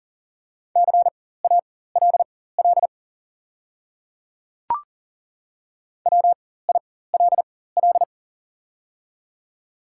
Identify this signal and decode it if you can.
Morse code, heard as a keyed tone, about 40 wpm, spaced as for 19 wpm, 700 Hz. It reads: CALL WILL
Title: 4x Character Spacing - Top 100 Words - 40wpm